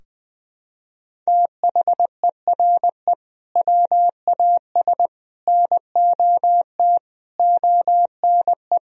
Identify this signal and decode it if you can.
Morse code, heard as a keyed tone, about 20 wpm, 700 Hz. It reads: THERE WAS NOT ONE
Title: Sentences from Top 1000 Words - Review Part 2 - 20wpm